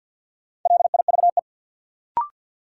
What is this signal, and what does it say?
Morse code, heard as a keyed tone, about 53 wpm, 700 Hz. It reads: LIVE
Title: Sentences from Top 200 Words - New Word Drill - 50wpm